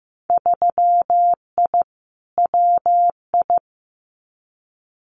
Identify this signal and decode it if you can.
Morse code, heard as a keyed tone, about 15 wpm, 700 Hz. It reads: 3I WI
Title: Winter Field Day Exchanges 15wpm